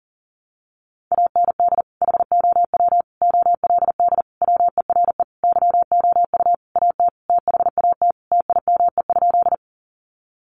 Morse code, heard as a keyed tone, about 40 wpm, 700 Hz. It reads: AND HOW OLD WERE YOU AT THAT TIME?
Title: Sentences from Top 200 Words 40wpm